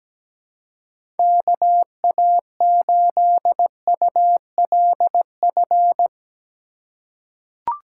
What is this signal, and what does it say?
Morse code, heard as a keyed tone, about 17 wpm, 700 Hz. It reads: KA8ULF
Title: USA Call Signs - 6 Characters - 4x Character Spacing - 17wpm